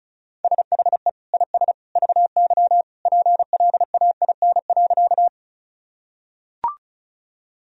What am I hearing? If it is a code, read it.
Morse code, heard as a keyed tone, about 36 wpm, 700 Hz. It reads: SHE IS VY PLAIN.
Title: Sentences from Top 400 Words and Abbreviations 35wpm